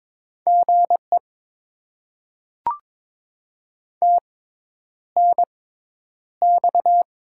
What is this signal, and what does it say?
Morse code, heard as a keyed tone, about 22 wpm, spaced as for 10 wpm, 690 Hz. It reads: GE TNX
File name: Top 20 QSO Elements - 6x Character Spacing - 22wpm